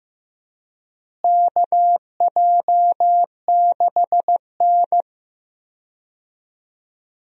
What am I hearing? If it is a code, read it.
Morse code, heard as a keyed tone, about 14 wpm, 700 Hz. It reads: KJ6N
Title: USA Call Signs - 4 Characters - Repeated 3x - 15wpm